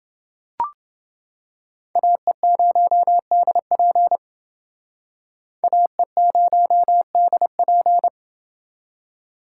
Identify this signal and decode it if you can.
Morse code, heard as a keyed tone, about 29 wpm, 700 Hz. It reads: AE0DP AE0DP